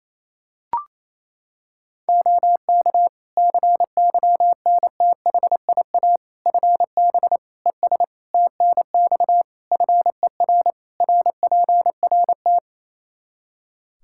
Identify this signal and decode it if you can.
Morse code, heard as a keyed tone, about 27 wpm, 700 Hz. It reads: OK CYNTHIA FB ES TNX FER RPRT
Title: K1USN SST QSO Practice 28wpm